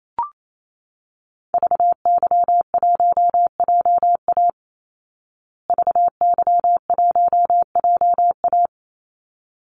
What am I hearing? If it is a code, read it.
Morse code, heard as a keyed tone, about 28 wpm, 700 Hz. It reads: VY1JA VY1JA
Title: Call Signs Repeated 3x - 28wpm